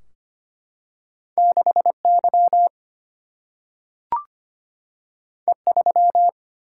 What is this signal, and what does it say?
Morse code, heard as a keyed tone, about 25 wpm, 700 Hz. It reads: 6Y E3
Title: Single-Letter Single-Number 25wpm